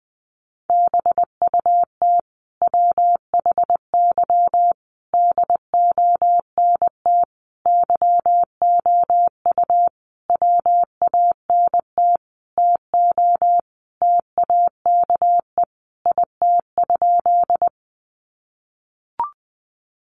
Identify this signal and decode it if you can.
Morse code, heard as a keyed tone, about 20 wpm, 700 Hz. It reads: BUT WHY DONT YOU WANT TO TAKE IT?
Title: Sentences from Top 300 Words 20wpm